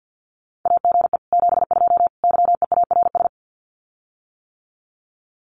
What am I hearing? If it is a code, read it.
Morse code, heard as a keyed tone, about 50 wpm, 700 Hz. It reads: AGE 72 YEARS
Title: Common QSO Phrases Encore 50wpm